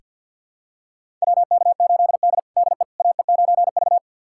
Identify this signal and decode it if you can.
Morse code, heard as a keyed tone, about 46 wpm, 700 Hz. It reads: WK8D DE AE9V
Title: USA Callsign DE Callsign - 4 Characters - 50wpm